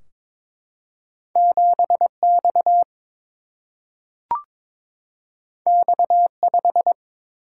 Morse code, heard as a keyed tone, about 22 wpm, 700 Hz. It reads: 7X X5
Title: Single-Letter Single-Number 22wpm